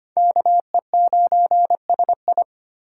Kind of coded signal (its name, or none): Morse code